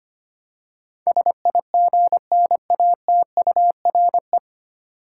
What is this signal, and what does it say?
Morse code, heard as a keyed tone, about 25 wpm, 700 Hz. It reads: SIGNATURE
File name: Nine Letter Words - 25wpm